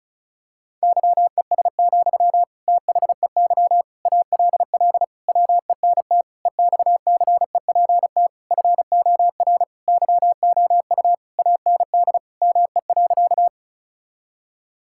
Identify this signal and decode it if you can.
Morse code, heard as a keyed tone, about 34 wpm, 700 Hz. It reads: YES, THEY ALL WENT EXCEPT FOR YOU AND ME.
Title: Sentences from Top 1600 Words 35wpm